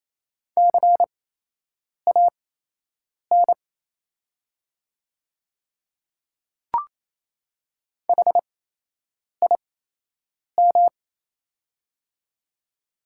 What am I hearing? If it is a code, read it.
Morse code, heard as a keyed tone, about 28 wpm, spaced as for 8 wpm, 700 Hz. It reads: CAN HIM
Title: 8x Character Spacing - Top 100 Words - 28wpm